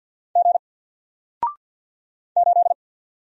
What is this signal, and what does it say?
Morse code, heard as a keyed tone, about 48 wpm, 700 Hz. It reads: G 8
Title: Single Character (Letter:Number:Punctuation:Prosign) Alt 50wpm